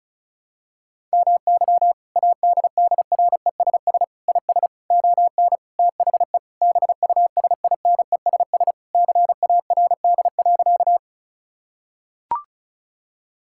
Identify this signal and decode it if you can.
Morse code, heard as a keyed tone, about 36 wpm, 700 Hz. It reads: MY ADDRESS IS ON THE BUSINESS CARD.
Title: Sentences from Top 1000 Words - Review Part 6 - 35wpm